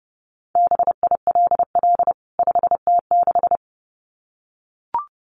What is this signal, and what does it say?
Morse code, heard as a keyed tone, about 30 wpm, 700 Hz. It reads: BILL 5T6
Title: CWT Exchange 30wpm